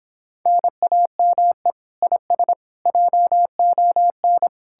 Morse code, heard as a keyed tone, about 31 wpm, 700 Hz. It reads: NAME IS JON